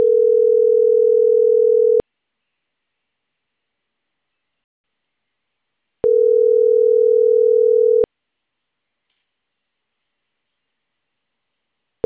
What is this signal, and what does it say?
Ringback tone (North America), 440 + 480 Hz.